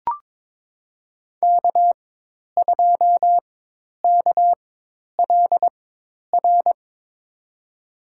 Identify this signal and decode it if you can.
Morse code, heard as a keyed tone, about 21 wpm, spaced as for 10 wpm, 700 Hz. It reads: K2KLR
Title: USA Call Signs - 4x Character Spacing - 22wpm